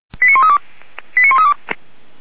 Five-tone selective call (ZVEI), heard as quick runs of five tones, tones of about 70 ms. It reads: 89132 89132